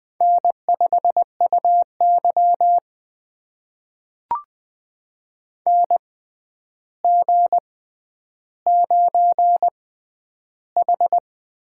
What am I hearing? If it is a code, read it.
Morse code, heard as a keyed tone, about 20 wpm, spaced as for 10 wpm, 700 Hz. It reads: N5UY NG9H
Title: USA Call Signs - 4 Characters - 6x Character Spacing - 20wpm